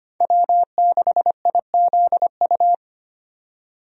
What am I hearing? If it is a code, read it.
Morse code, heard as a keyed tone, about 25 wpm, 700 Hz. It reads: W6IZU
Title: USA Call Signs - 5 Characters - 25wpm